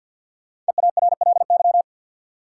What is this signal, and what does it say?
Morse code, heard as a keyed tone, about 53 wpm, 700 Hz. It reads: EARLY
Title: Sentences from Top 400 Words - New Word Drill - 50wpm